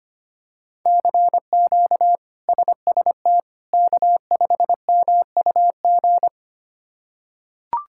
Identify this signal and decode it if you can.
Morse code, heard as a keyed tone, about 25 wpm, 700 Hz. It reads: CQ SST K5MUG
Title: K1USN SST QSO Practice - 25wpm